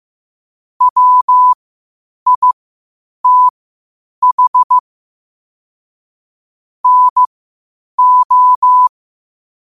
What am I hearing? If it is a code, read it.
Morse code, heard as a keyed tone, about 15 wpm, spaced as for 8 wpm, 1000 Hz. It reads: WITH NO
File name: qotdfiles / 8wpm